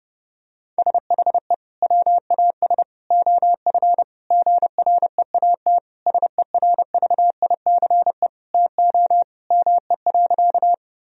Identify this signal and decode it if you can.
Morse code, heard as a keyed tone, about 32 wpm, 700 Hz. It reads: SHE WAS OF GREAT SERVICE TO ME.
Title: Sentences from Top 1000 Words - Review Part 4 - 30wpm